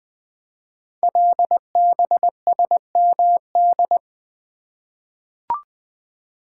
Morse code, heard as a keyed tone, about 20 wpm, 700 Hz. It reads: LBSMD
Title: Code Groups - 5 Letters - 20wpm